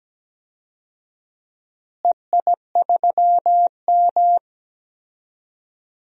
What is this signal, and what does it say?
Morse code, heard as a keyed tone, about 17 wpm, 700 Hz. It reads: EI3M
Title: DXCC-100 Call Signs - 4 Characters - Repeated 3x - 17wpm